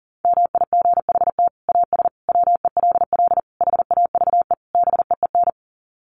Morse code, heard as a keyed tone, about 37 wpm, 700 Hz. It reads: MIGHT AS WELL HAVE BEEN
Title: Sets of 5 Words Encore2 40wpm